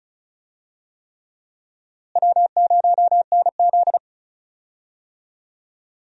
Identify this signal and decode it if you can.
Morse code, heard as a keyed tone, about 36 wpm, 700 Hz. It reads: W0NZ
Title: USA Call Signs - 4 Characters - Repeated 3x - 35wpm